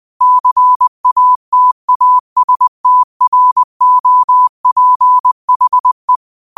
Morse code, heard as a keyed tone, about 20 wpm, 1000 Hz. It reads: CATASTROPHE
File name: qotdfiles / 20wpm